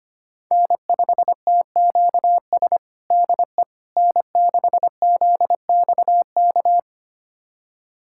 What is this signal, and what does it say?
Morse code, heard as a keyed tone, about 25 wpm, 700 Hz. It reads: N5TQS DE N6ZXK
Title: USA Callsign DE Callsign - 5 Characters - 25wpm